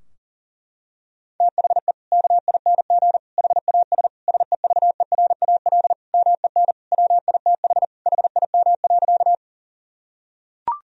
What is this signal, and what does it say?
Morse code, heard as a keyed tone, about 40 wpm, 700 Hz. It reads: THE KING HAS SEVERAL MEN WITH HIM.
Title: Sentences from Top 1000 Words - Review Part 2 - 40wpm